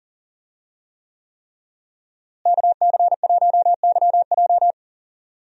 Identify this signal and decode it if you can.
Morse code, heard as a keyed tone, about 40 wpm, 700 Hz. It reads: KC1YJ
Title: USA Call Signs - 5 Characters - Repeated 3x - 40wpm